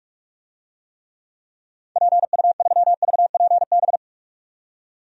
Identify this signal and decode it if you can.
Morse code, heard as a keyed tone, about 47 wpm, 700 Hz. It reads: PU3VPB
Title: DXCC-100 Call Signs - 6 Characters - 45wpm